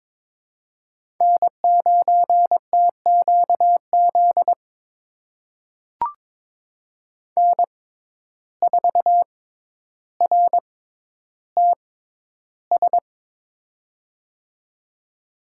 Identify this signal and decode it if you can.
Morse code, heard as a keyed tone, about 23 wpm, spaced as for 11 wpm, 700 Hz. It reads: N9TQZ N4RTS
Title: USA Call Signs - 5 Characters - 6x Character Spacing - 22wpm